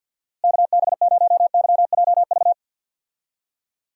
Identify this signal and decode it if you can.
Morse code, heard as a keyed tone, about 47 wpm, 700 Hz. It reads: KB0YJV
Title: USA Call Signs - 6 Characters - Repeated 2x - 50wpm